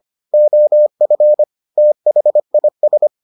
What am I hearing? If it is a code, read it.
Morse code, heard as a keyed tone, about 25 wpm, 600 Hz. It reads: OF THIS